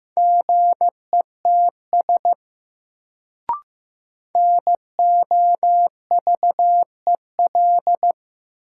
Morse code, heard as a keyed tone, about 15 wpm, 700 Hz. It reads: GETS NOVEL
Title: Sentences from Top 1300 Words - New Word Drill Alt - 15wpm